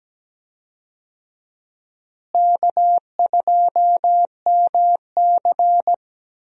Morse code, heard as a keyed tone, about 17 wpm, 700 Hz. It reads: K2MC
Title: USA Call Signs - 4 Characters - Repeated 2x - 17wpm